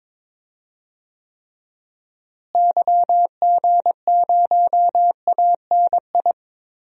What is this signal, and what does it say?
Morse code, heard as a keyed tone, about 22 wpm, 700 Hz. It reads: YG0ANI
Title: DXCC-100 Call Signs - 6 Characters - Repeated 3x - 22wpm